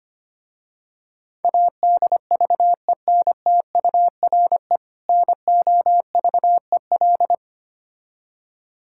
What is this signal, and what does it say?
Morse code, heard as a keyed tone, about 25 wpm, 700 Hz. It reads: ADVENTURE NOVEL
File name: practice / 25wpm / Difficult Words with Hints - 25wpm